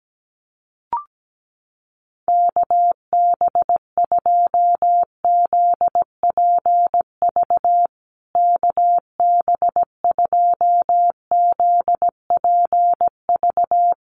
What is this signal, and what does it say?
Morse code, heard as a keyed tone, about 17 wpm, 700 Hz. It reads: KB2ZPV KB2ZPV
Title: USA Call Signs Repeated 2x 17wpm